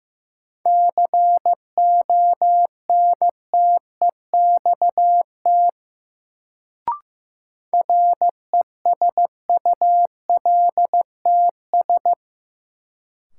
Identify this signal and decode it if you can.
Morse code, heard as a keyed tone, about 15 wpm, 700 Hz. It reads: CONTEXT RESULTS
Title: Sets of 4 Words - New Word Drill - 15wpm